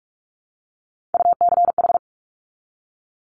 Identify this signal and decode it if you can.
Morse code, heard as a keyed tone, about 46 wpm, 700 Hz. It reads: UCH
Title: Top Three Letter Combinations in Words 45wpm